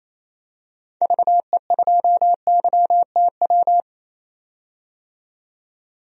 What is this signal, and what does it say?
Morse code, heard as a keyed tone, about 28 wpm, 700 Hz. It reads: VE2YTW
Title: DXCC-100 Call Signs - 6 Characters - 28wpm